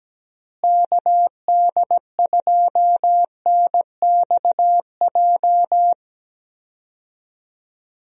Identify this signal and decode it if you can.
Morse code, heard as a keyed tone, about 17 wpm, 700 Hz. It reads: KD2NXJ
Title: USA Call Signs - 6 Characters - Repeated 2x - 17wpm